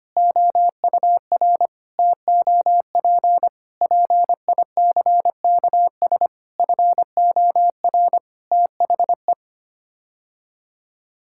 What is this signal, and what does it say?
Morse code, heard as a keyed tone, about 25 wpm, 700 Hz. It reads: OUR TOP PICKS FOR THE
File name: Sets of 9 Words - 25wpm